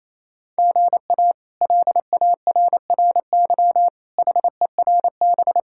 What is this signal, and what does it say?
Morse code, heard as a keyed tone, about 28 wpm, 700 Hz. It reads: GA LARRY HERB
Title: K1USN SST QSO Practice - 28wpm